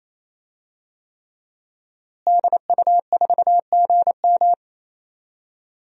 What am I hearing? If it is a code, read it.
Morse code, heard as a keyed tone, about 28 wpm, 700 Hz. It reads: DU4GM
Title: DXCC-100 Call Signs - 5 Characters - Repeated 3x - 28wpm